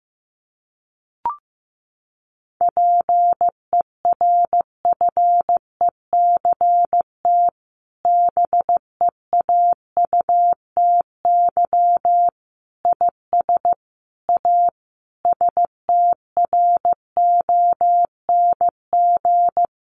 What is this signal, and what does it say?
Morse code, heard as a keyed tone, about 15 wpm, 700 Hz. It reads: PERFECT BEAUTY IS A STRONG
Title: Sentences from Top 1200 Words 15wpm